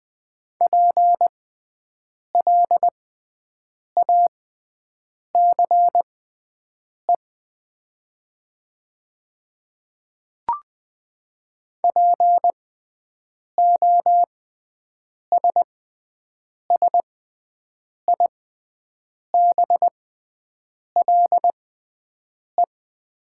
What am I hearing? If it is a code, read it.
Morse code, heard as a keyed tone, about 20 wpm, spaced as for 7 wpm, 700 Hz. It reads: PLACE POSSIBLE
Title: 6x Character Spacing - Top 500 Words - 20wpm